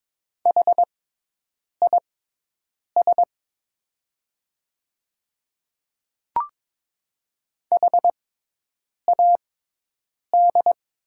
Morse code, heard as a keyed tone, about 22 wpm, spaced as for 8 wpm, 700 Hz. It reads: HIS HAD
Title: 6x Character Spacing - Top 200 Words - 22wpm